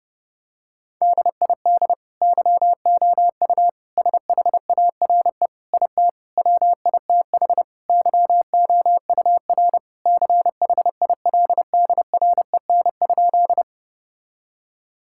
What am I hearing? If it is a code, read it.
Morse code, heard as a keyed tone, about 32 wpm, 700 Hz. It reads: DID YOU SHARE IT WITH YOUR CHILDREN?
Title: Sentences from Top 1000 Words - Review Part 6 - 30wpm